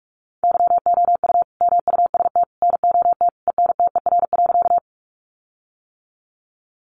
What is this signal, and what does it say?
Morse code, heard as a keyed tone, about 44 wpm, 700 Hz. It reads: YOU MUST NOT ENTER.
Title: Sentences from Top 1300 Words 45wpm